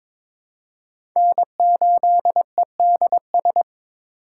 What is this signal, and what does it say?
Morse code, heard as a keyed tone, about 21 wpm, 700 Hz. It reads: N8EDS
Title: USA Call Signs - 5 Characters - 2x Character Spacing - 22wpm